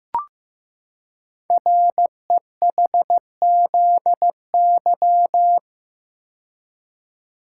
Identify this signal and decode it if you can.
Morse code, heard as a keyed tone, about 15 wpm, 700 Hz. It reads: REHZY
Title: Code Groups - 5 Letters - 15wpm